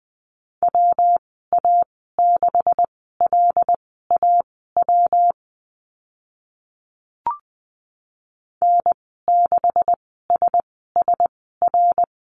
Morse code, heard as a keyed tone, about 20 wpm, spaced as for 14 wpm, 700 Hz. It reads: WA6LAW N6SSR
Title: 140 Lesson 23g - Letter f - Call Signs 20wpm